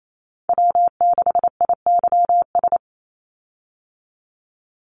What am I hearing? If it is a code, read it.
Morse code, heard as a keyed tone, about 28 wpm, 700 Hz. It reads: W6IYS
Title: Call Signs Repeated 2x - 28wpm